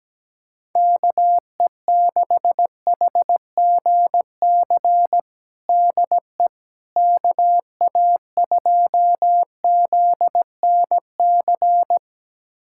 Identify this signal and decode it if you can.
Morse code, heard as a keyed tone, about 17 wpm, 700 Hz. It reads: KE6HGC DE KA2ZNC
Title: USA Callsign DE Callsign - 6 Characters - 17wpm